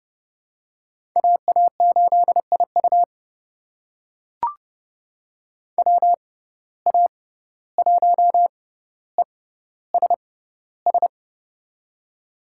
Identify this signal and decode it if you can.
Morse code, heard as a keyed tone, about 30 wpm, spaced as for 15 wpm, 700 Hz. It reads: AA8IU WA1ESS
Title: USA Call Signs - 6x Character Spacing - 30wpm